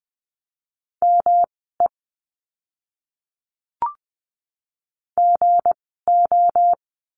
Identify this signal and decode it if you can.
Morse code, heard as a keyed tone, about 20 wpm, spaced as for 14 wpm, 700 Hz. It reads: ME GO
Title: 237 Lesson 38f - Letter q - Words 20wpm